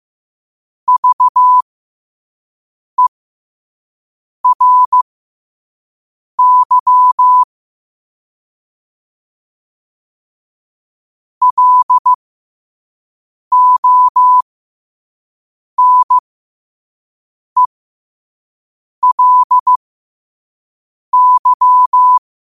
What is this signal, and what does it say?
Morse code, heard as a keyed tone, about 15 wpm, spaced as for 5 wpm, 1000 Hz. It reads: VERY LONELY
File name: qotdfiles / 5wpm